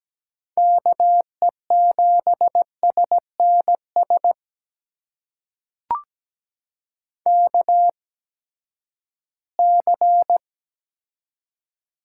Morse code, heard as a keyed tone, about 17 wpm, spaced as for 12 wpm, 700 Hz. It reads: KE7SNS KC
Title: USA Call Signs - 6 Characters - 8x Character Spacing - 17wpm